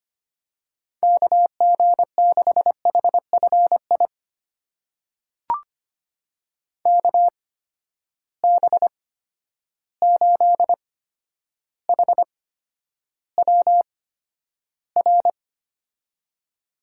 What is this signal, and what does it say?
Morse code, heard as a keyed tone, about 25 wpm, spaced as for 11 wpm, 700 Hz. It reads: KG6HFI KB8HWR
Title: USA Call Signs - 6 Characters - 8x Character Spacing - 25wpm